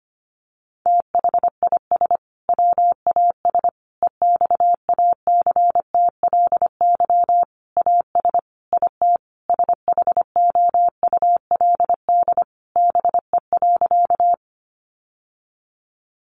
Morse code, heard as a keyed tone, about 25 wpm, 700 Hz. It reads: THIS WAS EXACTLY AS IT SHOULD BE.
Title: Sentences from Top 900 Words 25wpm